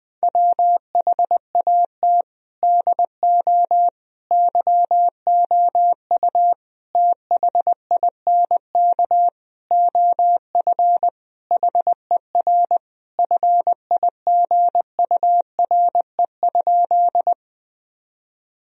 Morse code, heard as a keyed tone, about 20 wpm, 700 Hz. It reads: WHAT DO YOU THINK OF HER FIGURE?
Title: Sentences from Top 1000 Words - Review Part 2 - 20wpm